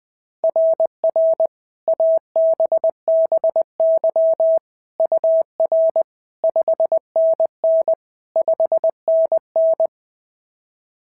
Morse code, heard as a keyed tone, about 21 wpm, 690 Hz. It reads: RR ABBY UR 5NN 5NN
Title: SOTA QSO Practice 20wpm